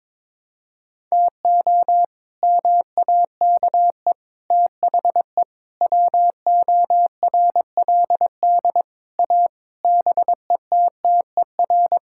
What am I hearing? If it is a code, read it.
Morse code, heard as a keyed tone, about 22 wpm, 700 Hz. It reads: TO MAKE THE WORLD A BETTER
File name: practice / 22wpm / Sets of 7 Words - 22wpm